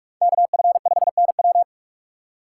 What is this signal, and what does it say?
Morse code, heard as a keyed tone, about 58 wpm, 700 Hz. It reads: KF5NW